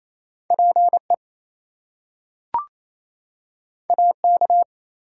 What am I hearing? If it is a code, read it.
Morse code, heard as a keyed tone, about 28 wpm, 700 Hz. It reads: PE AK
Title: US States and Canadian Provinces 28wpm